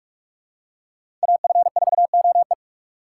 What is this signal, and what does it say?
Morse code, heard as a keyed tone, about 48 wpm, 700 Hz. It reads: AF4OE
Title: USA Call Signs - 5 Characters - 2x Character Spacing - 45wpm